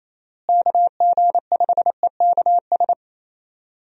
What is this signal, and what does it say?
Morse code, heard as a keyed tone, about 28 wpm, 700 Hz. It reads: KG5EKS
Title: USA Call Signs - 6 Characters - Repeated 2x - 28wpm